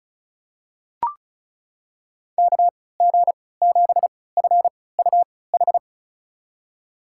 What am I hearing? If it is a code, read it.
Morse code, heard as a keyed tone, about 36 wpm, spaced as for 20 wpm, 700 Hz. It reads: KG7FUH
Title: USA Call Signs - 6 Characters - 3x Character Spacing - 35wpm